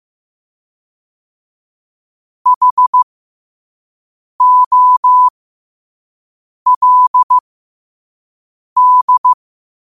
Morse code, heard as a keyed tone, about 15 wpm, spaced as for 5 wpm, 1000 Hz. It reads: HOLD
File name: qotdfiles / 5wpm